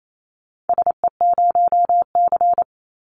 Morse code, heard as a keyed tone, about 28 wpm, 700 Hz. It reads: SE0C
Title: Call Signs Repeated 2x - 28wpm